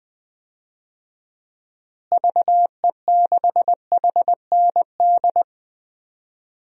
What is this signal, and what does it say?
Morse code, heard as a keyed tone, about 20 wpm, 700 Hz. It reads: VE6HND
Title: DXCC-100 Call Signs - 6 Characters - Repeated 2x - 20wpm